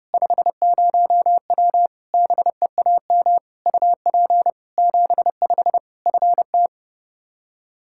Morse code, heard as a keyed tone, about 32 wpm, 700 Hz. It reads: 50W BEAM UP 75 FT
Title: Practice QSO - Exchange 2 - Info - 30wpm